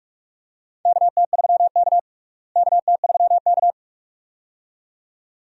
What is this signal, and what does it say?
Morse code, heard as a keyed tone, about 46 wpm, 700 Hz. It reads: KT3K KT3K
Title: USA Call Signs - 4 Characters - Repeated 2x - 45wpm